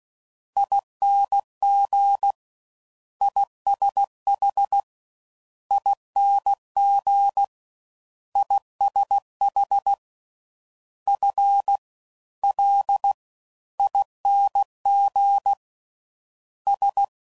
Morse code, heard as a keyed tone, about 16 wpm, 800 Hz. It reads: ING ISH ING ISH F L ING S